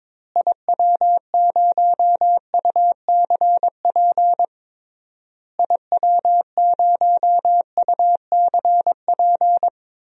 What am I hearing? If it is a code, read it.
Morse code, heard as a keyed tone, about 22 wpm, 700 Hz. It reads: IW0UCP IW0UCP
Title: DXCC-100 Call Signs - 6 Characters - Repeated 3x - 22wpm